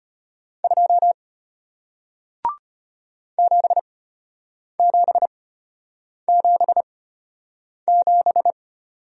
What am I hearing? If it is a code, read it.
Morse code, heard as a keyed tone, about 33 wpm, spaced as for 8 wpm, 700 Hz. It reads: 2 7777